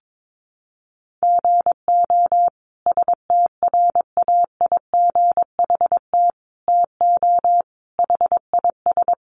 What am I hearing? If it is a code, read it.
Morse code, heard as a keyed tone, about 22 wpm, 700 Hz. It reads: GO STRAIGHT TO HIS
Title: Sentences from Top 1000 Words 22wpm